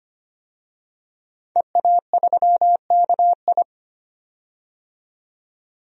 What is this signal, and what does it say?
Morse code, heard as a keyed tone, about 25 wpm, 700 Hz. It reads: EA3KI
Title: DXCC-100 Call Signs - 5 Characters - 25wpm